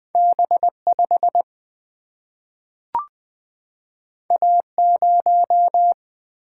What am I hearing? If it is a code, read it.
Morse code, heard as a keyed tone, about 20 wpm, 700 Hz. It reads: B 5 A 0
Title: Single-Letter Single-Number 20wpm